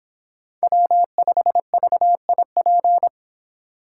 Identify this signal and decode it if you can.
Morse code, heard as a keyed tone, about 25 wpm, 700 Hz. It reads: W5VIP